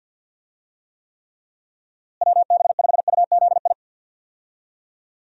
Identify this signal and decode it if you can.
Morse code, heard as a keyed tone, about 52 wpm, 700 Hz. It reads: WB5UZI
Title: USA Call Signs - 6 Characters - Repeated 3x - 50wpm